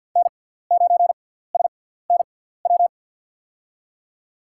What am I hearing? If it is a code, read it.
Morse code, heard as a keyed tone, about 47 wpm, spaced as for 17 wpm, 700 Hz. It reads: N9SNW
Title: USA Call Signs - 5 Characters - 6x Character Spacing - 50wpm